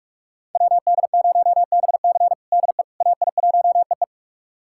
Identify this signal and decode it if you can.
Morse code, heard as a keyed tone, about 43 wpm, 700 Hz. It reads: WD0BC DE AI1EE
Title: USA Callsign DE Callsign - 5 Characters - 45wpm